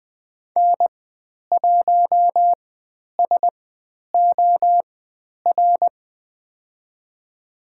Morse code, heard as a keyed tone, about 20 wpm, spaced as for 10 wpm, 700 Hz. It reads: N1SOR
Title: 109 Lesson 18g - Letter m - Call Signs 20wpm